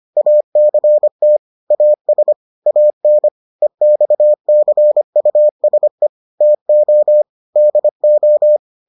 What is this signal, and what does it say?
Morse code, heard as a keyed tone, about 25 wpm, 600 Hz. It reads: ACT AS AN EXCUSE TO DO